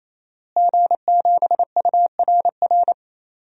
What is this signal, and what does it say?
Morse code, heard as a keyed tone, about 28 wpm, 690 Hz. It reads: G7URR
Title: DXCC-100 Call Signs - 5 Characters - Repeated 3x - 28wpm